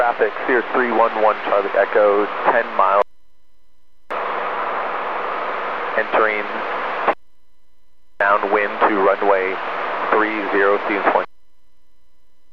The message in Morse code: none

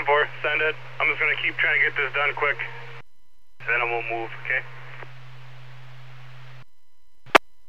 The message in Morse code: none

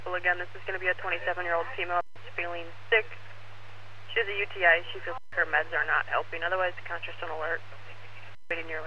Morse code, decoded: none

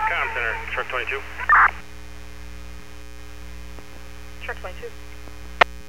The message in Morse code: none

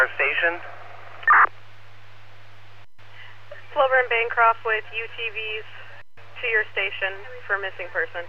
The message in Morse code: none

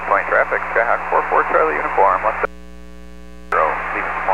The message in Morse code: none

